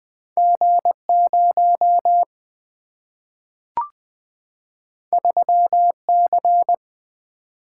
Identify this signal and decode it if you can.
Morse code, heard as a keyed tone, about 20 wpm, 700 Hz. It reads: G0 3C
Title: Two Letter-Number Combinations - 20wpm